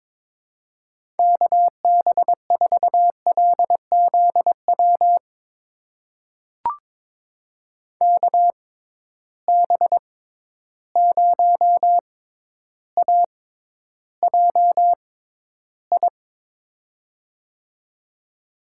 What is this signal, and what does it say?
Morse code, heard as a keyed tone, about 22 wpm, spaced as for 11 wpm, 700 Hz. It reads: KB4LZW KB0AJI
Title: USA Call Signs - 6 Characters - 6x Character Spacing - 22wpm